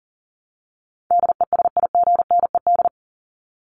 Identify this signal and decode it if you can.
Morse code, heard as a keyed tone, about 40 wpm, 700 Hz. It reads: DESIGNED
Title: ED Suffix - Top 100 Words with Base Words 40wpm